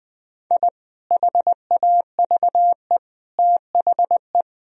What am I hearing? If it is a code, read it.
Morse code, heard as a keyed tone, about 20 wpm, 700 Hz. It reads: I HAVE THE
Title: Sentences from Top 1000 Words - Review Part 2 - 20wpm